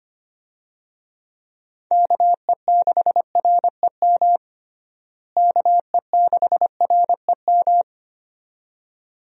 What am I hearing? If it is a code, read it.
Morse code, heard as a keyed tone, about 25 wpm, 700 Hz. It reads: KE6REM KE6REM
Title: USA Call Signs - 6 Characters - Repeated 2x - 25wpm